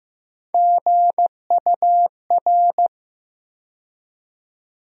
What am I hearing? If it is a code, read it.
Morse code, heard as a keyed tone, about 15 wpm, 700 Hz. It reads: GUR
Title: USA Call Sign Postfixes - 3 Letters 15wpm